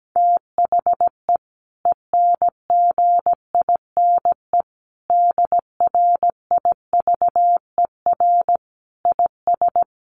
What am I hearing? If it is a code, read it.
Morse code, heard as a keyed tone, about 17 wpm, 700 Hz. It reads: THE ENGINE DRIVER IS
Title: Sentences from Top 1500 Words 17wpm